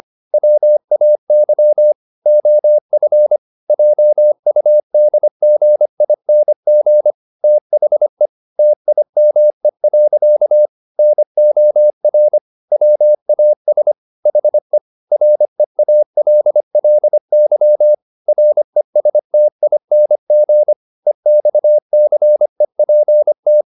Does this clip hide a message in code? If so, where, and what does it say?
Yes: Morse code from 0.3 to 23.6 s, heard as a keyed tone, about 25 wpm, 600 Hz. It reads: WAY OF JUDGING THE TIME. NOR WAS HE REALLY RESTING EXCEPT